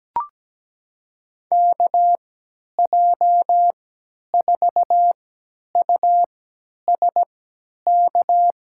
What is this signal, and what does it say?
Morse code, heard as a keyed tone, about 17 wpm, spaced as for 10 wpm, 700 Hz. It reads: KJ4USK
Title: USA Call Signs - 6 Characters - 3x Character Spacing - 17wpm